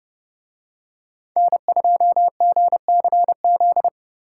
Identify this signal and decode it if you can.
Morse code, heard as a keyed tone, about 28 wpm, 700 Hz. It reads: N2GCZ